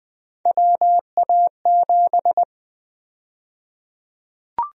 Morse code, heard as a keyed tone, about 20 wpm, 700 Hz. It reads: WA7
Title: USA Call Sign Prefixes - 20wpm